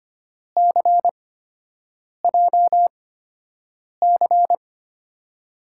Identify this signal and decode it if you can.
Morse code, heard as a keyed tone, about 25 wpm, spaced as for 7 wpm, 700 Hz. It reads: CJC
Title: USA Call Signs - 5 Characters - 8x Character Spacing - 25wpm